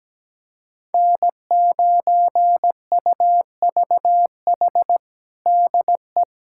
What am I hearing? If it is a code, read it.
Morse code, heard as a keyed tone, about 17 wpm, 700 Hz. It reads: N9UVH DE
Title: USA Callsign DE Callsign - 5 Characters - 17wpm